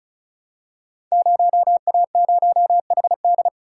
Morse code, heard as a keyed tone, about 36 wpm, 700 Hz. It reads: 0A0HD
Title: Code Groups - 5 Letter:Numbers - 35wpm